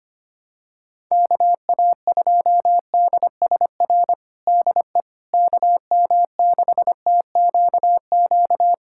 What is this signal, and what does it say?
Morse code, heard as a keyed tone, about 25 wpm, 700 Hz. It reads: KA2DSR DE KM6TQQ
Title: USA Callsign DE Callsign 25wpm